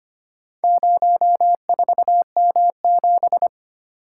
Morse code, heard as a keyed tone, about 25 wpm, 700 Hz. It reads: 04M7
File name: practice / 25wpm / Four Letter-Number Combinations - 25wpm